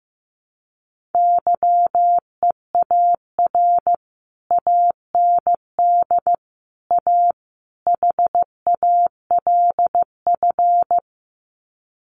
Morse code, heard as a keyed tone, about 15 wpm, 700 Hz. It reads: YEAR AND A HALF
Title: Sets of 4 Words 15wpm